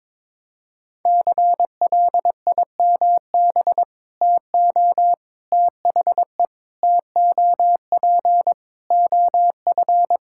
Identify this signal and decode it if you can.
Morse code, heard as a keyed tone, about 22 wpm, 700 Hz. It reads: CLIMB TO THE TOP OF
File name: practice / 22wpm / Sentences with 4-Word Phrases - 22wpm